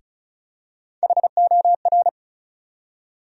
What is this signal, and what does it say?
Morse code, heard as a keyed tone, about 36 wpm, 700 Hz. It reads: HOR
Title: USA Call Sign Postfixes - 3 Letters 35wpm